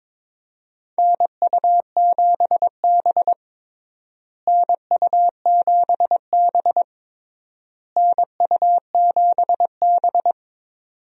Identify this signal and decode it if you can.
Morse code, heard as a keyed tone, about 21 wpm, 700 Hz. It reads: NU7B NU7B NU7B
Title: USA Call Signs - 4 Characters - Repeated 3x - 22wpm